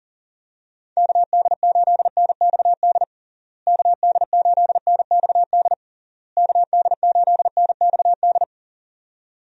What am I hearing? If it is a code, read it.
Morse code, heard as a keyed tone, about 40 wpm, 700 Hz. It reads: KD8NXD KD8NXD KD8NXD
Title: USA Call Signs - 6 Characters - Repeated 3x - 40wpm